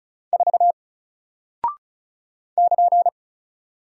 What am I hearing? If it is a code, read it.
Morse code, heard as a keyed tone, about 36 wpm, 700 Hz. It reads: 4 <KN>
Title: Single Character (Letter:Number:Punctuation:Prosign) Alt 35wpm